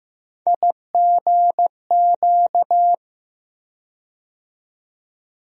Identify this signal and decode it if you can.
Morse code, heard as a keyed tone, about 15 wpm, 700 Hz. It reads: IGQ
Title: USA Call Sign Postfixes - 3 Letters 15wpm